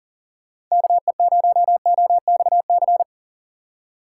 Morse code, heard as a keyed tone, about 40 wpm, 700 Hz. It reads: KE0OXC
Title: USA Call Signs - 6 Characters - 8x Character Spacing - 40wpm